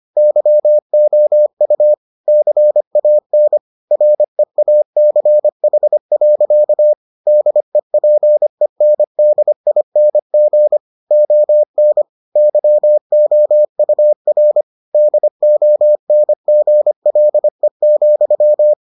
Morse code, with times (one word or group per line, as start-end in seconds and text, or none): 0.2-1.9 YOU
2.3-3.6 CAN
3.9-6.9 REACH.
7.3-10.8 DEPENDING
11.1-12.0 ON
12.4-14.6 YOUR
14.9-18.7 DONGLE,